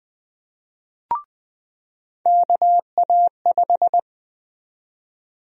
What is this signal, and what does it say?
Morse code, heard as a keyed tone, about 20 wpm, 700 Hz. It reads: KA5
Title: USA Call Sign Prefixes - 20wpm